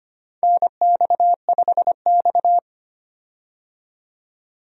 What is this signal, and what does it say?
Morse code, heard as a keyed tone, about 25 wpm, 700 Hz. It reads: NX5X
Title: USA Call Signs - 4 Characters - 25wpm